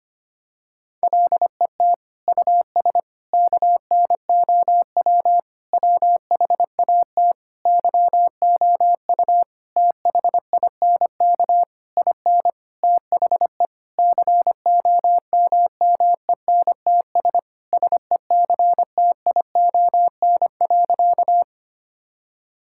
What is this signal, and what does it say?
Morse code, heard as a keyed tone, about 25 wpm, 700 Hz. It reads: LET US KNOW WHAT YOU THINK IN THE COMMENTS SECTION.
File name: practice / 25wpm / Sentences with 9-Word Phrases - 25wpm